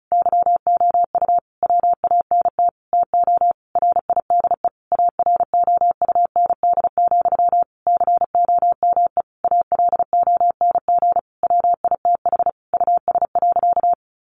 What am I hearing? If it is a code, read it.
Morse code, heard as a keyed tone, about 34 wpm, 700 Hz. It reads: YOU WANT TO RIDE AROUND, COME ALONG WITH US.
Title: Sentences from Top 1200 Words 35wpm